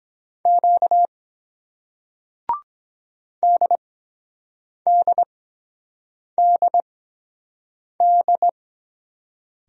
Morse code, heard as a keyed tone, about 23 wpm, spaced as for 7 wpm, 700 Hz. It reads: Q DDDD